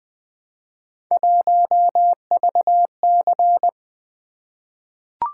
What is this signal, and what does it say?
Morse code, heard as a keyed tone, about 20 wpm, 700 Hz. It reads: 1VC
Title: Three Letter-Number Combinations - 20wpm